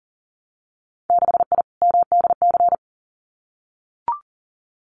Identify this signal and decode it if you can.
Morse code, heard as a keyed tone, about 42 wpm, 700 Hz. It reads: 6I MDC
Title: Winter Field Day Exchanges Encore 40wpm